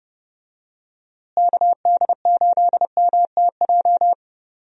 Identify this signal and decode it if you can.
Morse code, heard as a keyed tone, about 30 wpm, 700 Hz. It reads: KD8MTJ
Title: USA Call Signs - 6 Characters - Repeated 3x - 30wpm